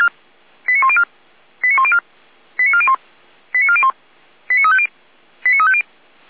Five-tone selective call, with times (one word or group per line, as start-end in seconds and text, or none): 0.7-1.0 89104
1.6-2.0 89104
2.6-2.9 89401
3.5-3.9 89401
4.5-4.9 89350
5.4-5.8 89350